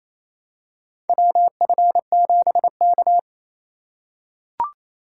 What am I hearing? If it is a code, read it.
Morse code, heard as a keyed tone, about 27 wpm, 700 Hz. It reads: WF7K
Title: USA Call Signs - 4 Characters - 4x Character Spacing - 28wpm